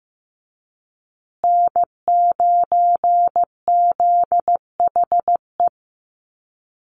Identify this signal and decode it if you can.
Morse code, heard as a keyed tone, about 15 wpm, 700 Hz. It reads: N9ZHE
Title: USA Call Signs Repeated 2x 15wpm